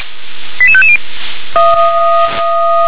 Five-tone selective call (ZVEI), heard as quick runs of five tones, tones of about 70 ms. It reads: 88550